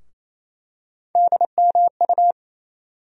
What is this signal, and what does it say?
Morse code, heard as a keyed tone, about 28 wpm, 700 Hz. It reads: DMU
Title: USA Call Sign Postfixes - 3 Letters 28wpm